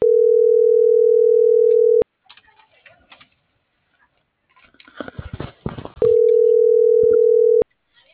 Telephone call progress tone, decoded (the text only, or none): ringback tone